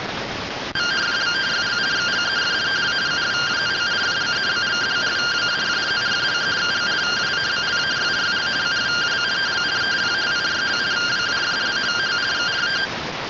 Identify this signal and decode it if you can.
Two-tone teleprinter signal, about 170 Hz shift.